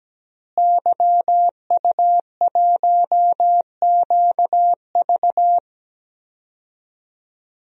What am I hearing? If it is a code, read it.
Morse code, heard as a keyed tone, about 17 wpm, 700 Hz. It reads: YU1QV
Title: DXCC-100 Call Signs - 5 Characters - Repeated 2x - 17wpm